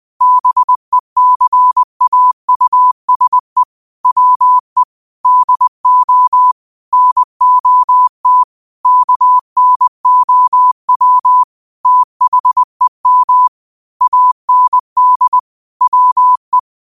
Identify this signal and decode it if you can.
Morse code, heard as a keyed tone, about 20 wpm, 1000 Hz. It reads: BECAUSE WE DO NOT KNOW THEM AND WE
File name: qotdfiles / 20wpm